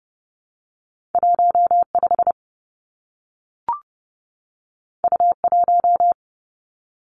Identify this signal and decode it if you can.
Morse code, heard as a keyed tone, about 30 wpm, 700 Hz. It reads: 1 5 U 1
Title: Two Challenging Letter-Number Combinations 30wpm